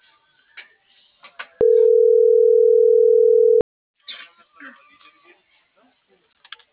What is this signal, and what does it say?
Ringback tone (North America), 440 + 480 Hz.